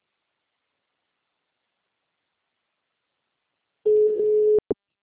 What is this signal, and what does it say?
Ringback tone (Europe, 425 Hz).
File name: out- -2017- - - .1298